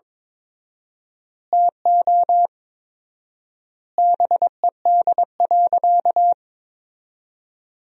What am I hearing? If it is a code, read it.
Morse code, heard as a keyed tone, about 21 wpm, 700 Hz. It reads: TO BED.
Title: 4x Word-Spacing - Sentences from Top 500 Words - 22wpm